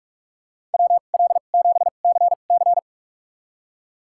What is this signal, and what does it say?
Morse code, heard as a keyed tone, about 46 wpm, spaced as for 33 wpm, 700 Hz. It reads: WL7CC
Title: USA Call Signs - 5 Characters - 2x Character Spacing - 45wpm